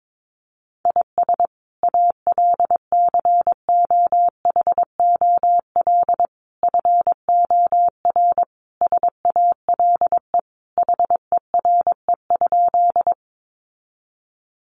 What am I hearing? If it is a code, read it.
Morse code, heard as a keyed tone, about 22 wpm, 700 Hz. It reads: IS ALCOHOL FOR SALE HERE?
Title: Sentences from Top 1400 Words 22wpm